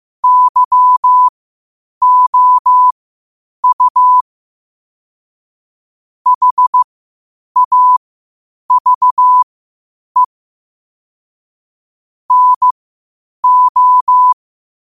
Morse code, heard as a keyed tone, about 15 wpm, spaced as for 8 wpm, 1000 Hz. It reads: YOU HAVE NO
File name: qotdfiles / 8wpm